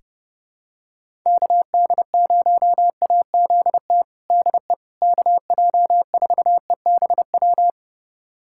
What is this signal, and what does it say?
Morse code, heard as a keyed tone, about 30 wpm, 700 Hz. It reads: KD0AZT DE KJ4EBW
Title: USA Callsign DE Callsign 30wpm